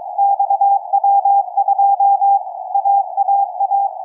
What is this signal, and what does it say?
Morse code, heard as a keyed tone, about 22 wpm, 800 Hz, with radio noise. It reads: XW2 AAA
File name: XW2A-CAS3A-VERT-02 17h46 utc 145mhz660 cw 2